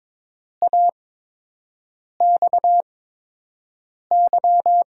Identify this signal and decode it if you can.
Morse code, heard as a keyed tone, about 22 wpm, spaced as for 6 wpm, 700 Hz. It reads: AXY
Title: USA Call Signs - 6 Characters - 8x Character Spacing - 22wpm